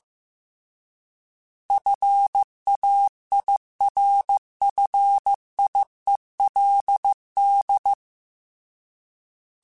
Morse code, heard as a keyed tone, about 15 wpm, 800 Hz. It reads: FAIRFIELD